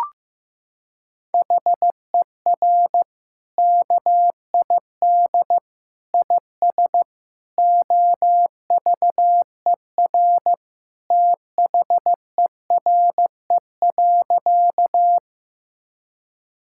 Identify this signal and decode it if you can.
Morse code, heard as a keyed tone, about 16 wpm, 700 Hz. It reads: HER KID IS OVER THERE.
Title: Sentences from Top 1600 Words 15wpm